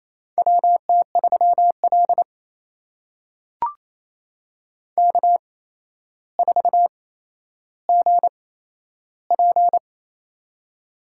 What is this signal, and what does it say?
Morse code, heard as a keyed tone, about 27 wpm, spaced as for 12 wpm, 700 Hz. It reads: WT3L K4GP
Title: USA Call Signs - 8x Character Spacing - 28wpm